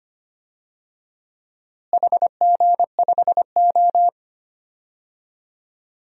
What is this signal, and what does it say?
Morse code, heard as a keyed tone, about 25 wpm, 700 Hz. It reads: HG5O
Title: DXCC-100 Call Signs - 4 Characters - Repeated 2x - 25wpm